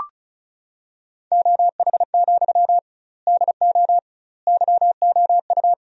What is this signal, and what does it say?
Morse code, heard as a keyed tone, about 34 wpm, 700 Hz. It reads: OH, DO YOU
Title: 2x Word-Spacing - Sentences from Top 500 Words - 35wpm